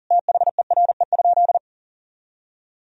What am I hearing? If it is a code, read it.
Morse code, heard as a keyed tone, about 40 wpm, 700 Hz. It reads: THERE?
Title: Sentences from Top 1000 Words - Review Part 3 - 40wpm